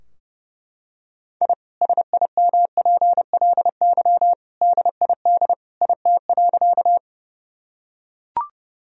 Morse code, heard as a keyed tone, about 30 wpm, 700 Hz. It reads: I SIMPLY DID IT.
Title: Sentences from Top 1000 Words - Review Part 4 - 30wpm